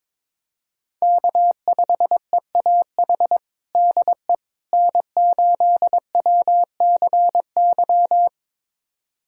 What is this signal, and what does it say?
Morse code, heard as a keyed tone, about 22 wpm, 700 Hz. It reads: K5EAH DE N8WCY
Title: USA Callsign DE Callsign - 5 Characters - 22wpm